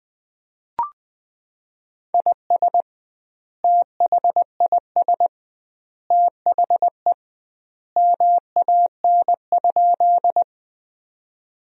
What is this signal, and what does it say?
Morse code, heard as a keyed tone, about 20 wpm, 700 Hz. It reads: IS THIS THE MAN?
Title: 2x Word-Spacing - Sentences from Top 200 Words - 20wpm